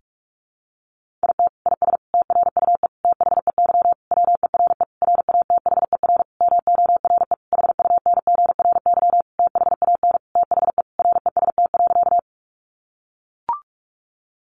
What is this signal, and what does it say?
Morse code, heard as a keyed tone, about 44 wpm, 700 Hz. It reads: IT IS TRUE THEY WERE RATHER MORE HUNGRY THAN THE REST.
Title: Sentences from Top 1400 Words 45wpm